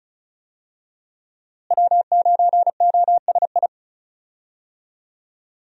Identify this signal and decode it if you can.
Morse code, heard as a keyed tone, about 36 wpm, 700 Hz. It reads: W9OSI
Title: USA Call Signs - 5 Characters - Repeated 3x - 35wpm